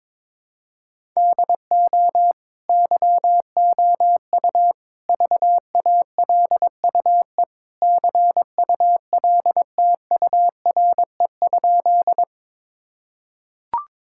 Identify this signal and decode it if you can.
Morse code, heard as a keyed tone, about 22 wpm, 700 Hz. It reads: DO YOU VALUE CULTURE?
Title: Sentences from Top 1000 Words - Review Part 6 - 22wpm